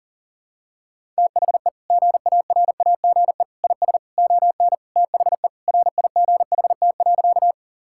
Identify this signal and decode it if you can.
Morse code, heard as a keyed tone, about 40 wpm, 700 Hz. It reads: THE GARAGE IS ON THE RIGHT.
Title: Sentences from Top 1600 Words 40wpm